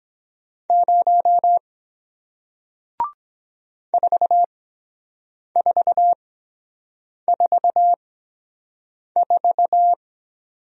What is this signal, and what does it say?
Morse code, heard as a keyed tone, about 23 wpm, spaced as for 7 wpm, 700 Hz. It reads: 0 4444